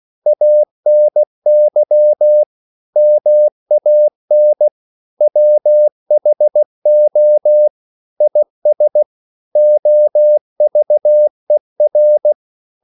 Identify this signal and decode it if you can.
Morse code, heard as a keyed tone, about 16 wpm, 600 Hz. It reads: ANY MAN WHO IS OVER
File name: CW / Practice / EnglishQuotations0216